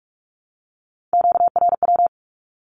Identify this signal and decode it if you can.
Morse code, heard as a keyed tone, about 46 wpm, 700 Hz. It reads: QRW
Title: Q-Codes with Spoken Meaning - 45wpm